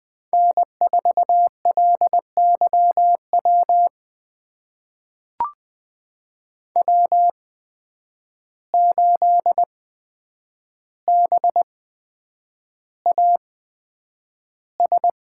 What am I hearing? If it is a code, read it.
Morse code, heard as a keyed tone, about 20 wpm, spaced as for 9 wpm, 700 Hz. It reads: N4LYW W8BAS
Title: USA Call Signs - 5 Characters - 8x Character Spacing - 20wpm